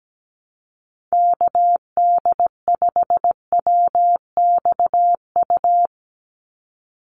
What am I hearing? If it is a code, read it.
Morse code, heard as a keyed tone, about 17 wpm, 700 Hz. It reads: KD5WXU
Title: USA Call Signs 17wpm